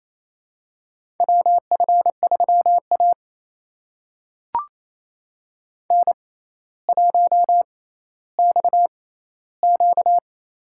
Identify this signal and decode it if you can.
Morse code, heard as a keyed tone, about 28 wpm, spaced as for 14 wpm, 700 Hz. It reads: WF3A N1XQ
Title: USA Call Signs - 4 Characters - 6x Character Spacing - 28wpm